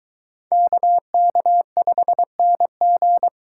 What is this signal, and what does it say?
Morse code, heard as a keyed tone, about 21 wpm, 700 Hz. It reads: KK5NG